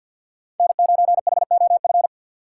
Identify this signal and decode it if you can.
Morse code, heard as a keyed tone, about 48 wpm, 700 Hz. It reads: N9HOF
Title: USA Call Signs - 5 Characters - 4x Character Spacing - 50wpm